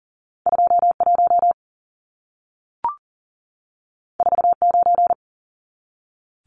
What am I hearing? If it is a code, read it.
Morse code, heard as a keyed tone, about 40 wpm, 700 Hz. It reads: 21 49
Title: Numbers - 2-Digit 40wpm